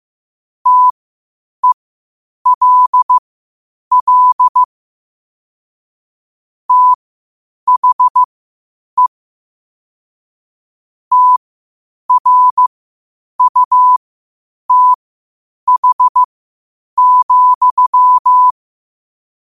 Morse code, heard as a keyed tone, about 15 wpm, spaced as for 8 wpm, 1000 Hz. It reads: TELL THE TRUTH,
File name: qotdfiles / 8wpm